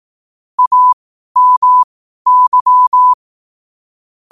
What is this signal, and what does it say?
Morse code, heard as a keyed tone, about 18 wpm, spaced as for 12 wpm, 1000 Hz. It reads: AMY
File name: qotdfiles / 12wpm